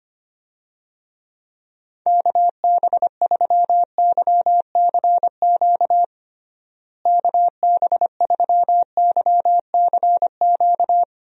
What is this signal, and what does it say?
Morse code, heard as a keyed tone, about 25 wpm, 700 Hz. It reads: KB3YCQ KB3YCQ
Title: USA Call Signs - 6 Characters - Repeated 3x - 25wpm